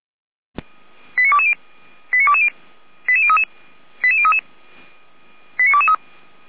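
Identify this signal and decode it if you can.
Five-tone selective call (ZVEI), heard as quick runs of five tones, tones of about 70 ms. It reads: 89220 89220 88033 88033 89223
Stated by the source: ZVEI five-tone selective call received on SDR.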